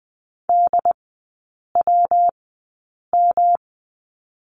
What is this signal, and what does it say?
Morse code, heard as a keyed tone, about 20 wpm, spaced as for 8 wpm, 690 Hz. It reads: DWM
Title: 145 Lesson 24e - Letter y - Three Characters 20wpm